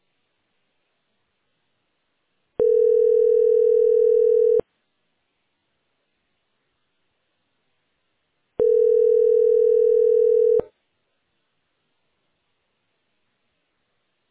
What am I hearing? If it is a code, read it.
Ringback tone (North America), 440 + 480 Hz.